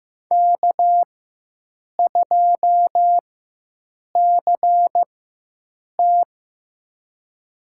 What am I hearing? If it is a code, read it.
Morse code, heard as a keyed tone, about 15 wpm, spaced as for 7 wpm, 700 Hz. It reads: K2CT
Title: USA Call Signs - 4 Characters - 4x Character Spacing - 15wpm